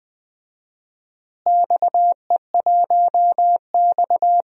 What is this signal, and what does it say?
Morse code, heard as a keyed tone, about 20 wpm, 700 Hz. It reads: XE1X
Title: DXCC-100 Call Signs - 4 Characters - Repeated 3x - 20wpm